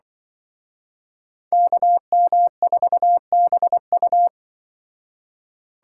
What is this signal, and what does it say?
Morse code, heard as a keyed tone, about 23 wpm, 700 Hz. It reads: KM4BU
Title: Speed Racing - USA 2x2 Call Signs 20wpm